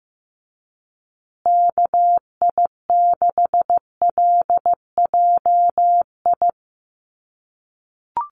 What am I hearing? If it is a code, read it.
Morse code, heard as a keyed tone, about 15 wpm, 700 Hz. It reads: KI6LJI
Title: USA Call Signs Repeated 2x 15wpm